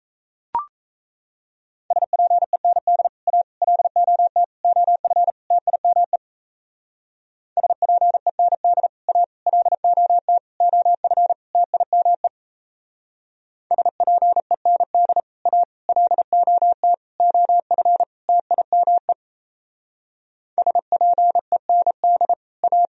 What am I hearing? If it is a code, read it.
Morse code, heard as a keyed tone, about 35 wpm, 700 Hz. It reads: SPEND A LOT OF TIME SPEND A LOT OF TIME SPEND A LOT OF TIME SPEND A